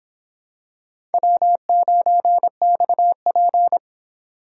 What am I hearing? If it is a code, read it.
Morse code, heard as a keyed tone, about 26 wpm, 700 Hz. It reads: W9XP